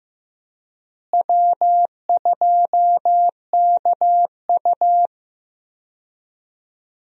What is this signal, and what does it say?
Morse code, heard as a keyed tone, about 15 wpm, 700 Hz. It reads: W2KU
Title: USA Call Signs - 4 Characters - Repeated 3x - 15wpm